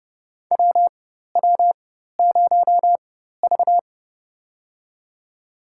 Morse code, heard as a keyed tone, about 30 wpm, spaced as for 14 wpm, 700 Hz. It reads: WW0V
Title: USA Call Signs - 4 Characters - 4x Character Spacing - 30wpm